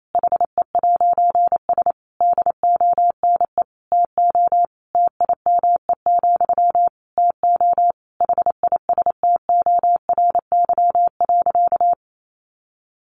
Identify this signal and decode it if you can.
Morse code, heard as a keyed tone, about 28 wpm, 700 Hz. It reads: HE'S DONE TO TIME, TO HISTORY.
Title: A Sound Of Thunder 28wpm